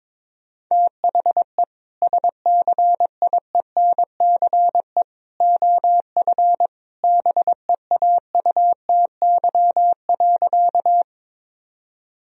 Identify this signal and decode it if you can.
Morse code, heard as a keyed tone, about 22 wpm, 700 Hz. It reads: THE SCIENCE OF BEAUTY.
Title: Sentences from Top 1000 Words - Review Part 2 - 22wpm